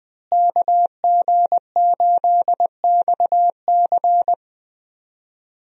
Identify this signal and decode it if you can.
Morse code, heard as a keyed tone, about 20 wpm, 700 Hz. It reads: KG8XC
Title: USA Call Signs - 5 Characters - Repeated 3x - 20wpm